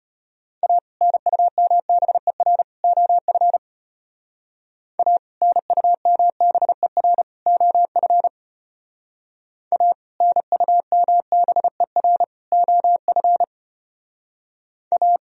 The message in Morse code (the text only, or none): A NUMBER OF A NUMBER OF A NUMBER OF A